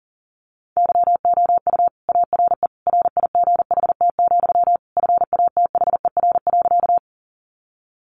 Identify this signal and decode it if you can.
Morse code, heard as a keyed tone, about 37 wpm, 700 Hz. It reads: YOU ARE RIGHT, FATHER.
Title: Sentences from Top 300 Words 40wpm